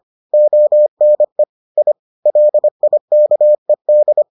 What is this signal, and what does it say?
Morse code, heard as a keyed tone, about 25 wpm, 600 Hz. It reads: ONE I LIKED